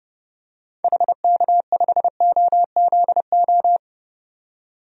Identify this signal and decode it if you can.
Morse code, heard as a keyed tone, about 32 wpm, 690 Hz. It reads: HK5OZO
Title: DXCC-100 Call Signs - 6 Characters - 30wpm